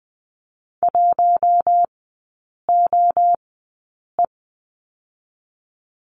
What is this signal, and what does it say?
Morse code, heard as a keyed tone, about 20 wpm, spaced as for 8 wpm, 700 Hz. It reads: 1OE
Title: 041 Lesson 8e - Number 1 - Three Characters 20wpm